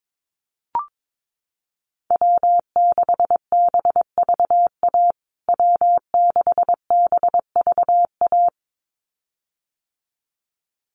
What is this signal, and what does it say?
Morse code, heard as a keyed tone, about 22 wpm, 700 Hz. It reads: W6BVA W6BVA
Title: USA Call Signs Repeated 2x 22wpm